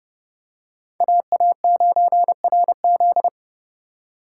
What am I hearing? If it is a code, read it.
Morse code, heard as a keyed tone, about 29 wpm, 700 Hz. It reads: AA9RZ